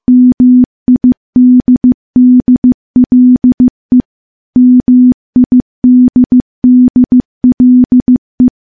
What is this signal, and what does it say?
Morse code, heard as a keyed tone, about 14 wpm, 250 Hz. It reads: MIDDLE MIDDLE
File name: morse / morse-training-15WPM